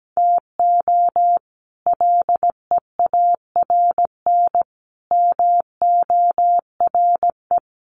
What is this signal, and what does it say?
Morse code, heard as a keyed tone, about 17 wpm, 700 Hz. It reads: TO LEARN MORE
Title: Sets of 5 Words Encore2 17wpm